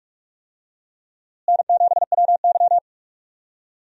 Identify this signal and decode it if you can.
Morse code, heard as a keyed tone, about 45 wpm, 700 Hz. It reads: N7WY
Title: USA Call Signs - 4 Characters - Repeated 3x - 45wpm